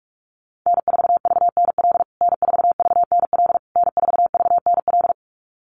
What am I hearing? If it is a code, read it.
Morse code, heard as a keyed tone, about 46 wpm, 690 Hz. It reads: N4VNL N4VNL N4VNL
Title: USA Call Signs Repeated 3x 45wpm